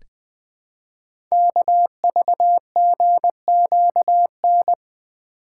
Morse code, heard as a keyed tone, about 20 wpm, 700 Hz. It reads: KVGQN
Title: Code Groups - 5 Letters - 20wpm